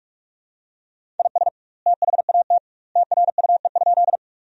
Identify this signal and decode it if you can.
Morse code, heard as a keyed tone, about 44 wpm, 700 Hz. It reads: IS THAT TRUE?
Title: 2x Word-Spacing - Sentences from Top 400 Words - 45wpm